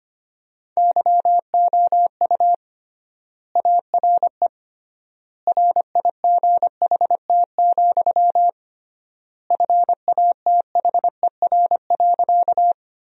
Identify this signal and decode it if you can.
Morse code, heard as a keyed tone, about 25 wpm, 700 Hz. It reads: YOU ARE RIGHT, FATHER.
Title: 3x Word-Spacing - Sentences from Top 300 Words - 25wpm